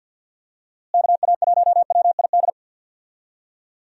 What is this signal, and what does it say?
Morse code, heard as a keyed tone, about 46 wpm, 700 Hz. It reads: KA1WID
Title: USA Call Signs - 6 Characters - Repeated 3x - 50wpm